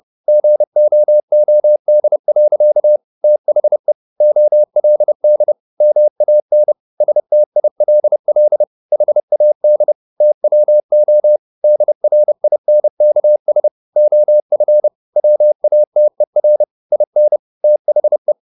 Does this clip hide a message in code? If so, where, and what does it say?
Yes: Morse code from 0.3 to 18.3 s, heard as a keyed tone, about 30 wpm, 600 Hz. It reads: GOOD. THE OLD MAN STILL HAD TWO DRINKS OF WATER IN THE